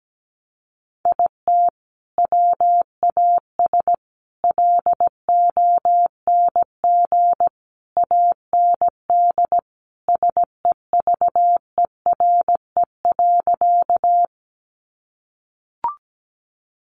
Morse code, heard as a keyed tone, about 17 wpm, 700 Hz. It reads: IT WAS LONG AND SEVERE.
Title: Sentences from Top 1400 Words 17wpm